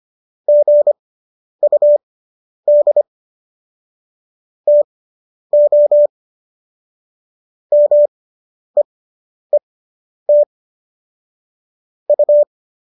Morse code, heard as a keyed tone, about 25 wpm, spaced as for 10 wpm, 590 Hz. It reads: GUD TO MEET U
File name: qso219 10